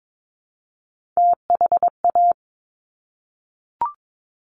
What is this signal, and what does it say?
Morse code, heard as a keyed tone, about 22 wpm, 700 Hz. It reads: THA
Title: Top Three Letter Combinations in Words 22wpm